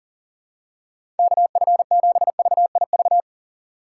Morse code, heard as a keyed tone, about 37 wpm, 700 Hz. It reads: KF7VIV
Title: USA Call Signs - 3x Character Spacing - 40wpm